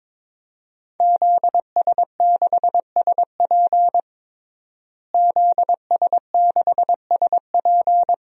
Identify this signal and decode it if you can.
Morse code, heard as a keyed tone, about 22 wpm, 700 Hz. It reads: ZS6SP ZS6SP
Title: DXCC-100 Call Signs - 5 Characters - Repeated 3x - 22wpm